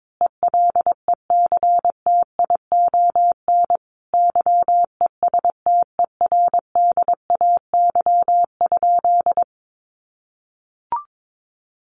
Morse code, heard as a keyed tone, about 22 wpm, 700 Hz. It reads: ELECTION YESTERDAY?
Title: A Sound Of Thunder 22wpm